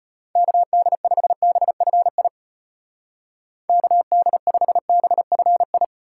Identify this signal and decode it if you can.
Morse code, heard as a keyed tone, about 31 wpm, 700 Hz. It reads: KD5BFI KD5BFI
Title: Speed Racing - USA 2x3 Call Signs 25wpm